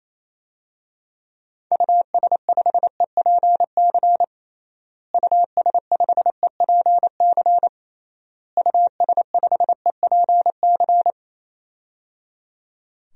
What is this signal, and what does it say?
Morse code, heard as a keyed tone, about 28 wpm, 700 Hz. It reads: US5EPC US5EPC US5EPC
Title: DXCC-100 Call Signs - 6 Characters - Repeated 3x - 28wpm